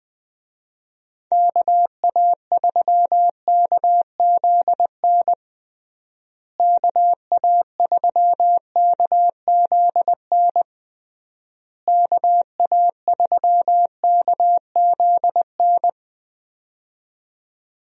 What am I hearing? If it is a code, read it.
Morse code, heard as a keyed tone, about 21 wpm, 700 Hz. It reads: KA3KZN KA3KZN KA3KZN
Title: USA Call Signs - 6 Characters - Repeated 3x - 20wpm